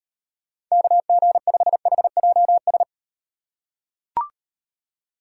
Morse code, heard as a keyed tone, about 38 wpm, 700 Hz. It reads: KG5HJS